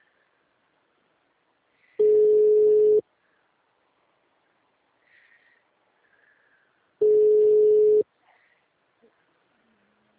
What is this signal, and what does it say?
Ringback tone (Europe, 425 Hz).